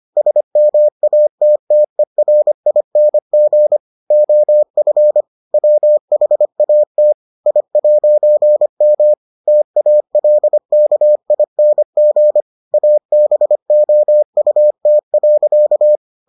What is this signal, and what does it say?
Morse code, heard as a keyed tone, about 25 wpm, 600 Hz. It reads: SMATTERING OF WHAT I'M TALKING ABOUT.